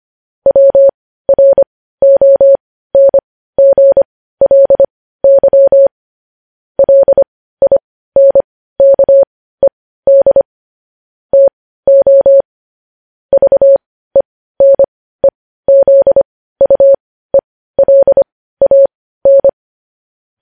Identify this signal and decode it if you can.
Morse code, heard as a keyed tone, about 25 wpm, spaced as for 15 wpm, 550 Hz. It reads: WRONGLY LINKED TO VENEZUELAN
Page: news-headlines-in-morse-code-at-15-wpm